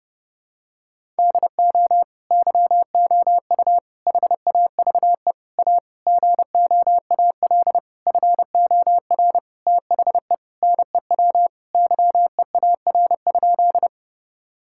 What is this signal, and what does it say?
Morse code, heard as a keyed tone, about 30 wpm, 700 Hz. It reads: DO YOU HAVE A GOAL FOR THE NEW YEAR?
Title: Sentences from Top 1000 Words - Review Part 5 - 30wpm